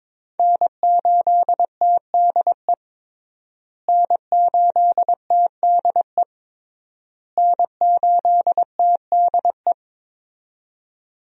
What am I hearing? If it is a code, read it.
Morse code, heard as a keyed tone, about 22 wpm, 700 Hz. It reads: N8TDE N8TDE N8TDE
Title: USA Call Signs - 5 Characters - Repeated 3x - 22wpm